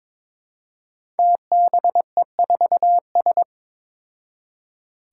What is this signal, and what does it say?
Morse code, heard as a keyed tone, about 22 wpm, 700 Hz. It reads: TBE4S
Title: Five Challenging Letter-Number Combinations 22wpm